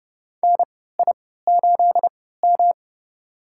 Morse code, heard as a keyed tone, about 30 wpm, spaced as for 17 wpm, 700 Hz. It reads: NI8M
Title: USA Call Signs - 4 Characters - 3x Character Spacing - 30wpm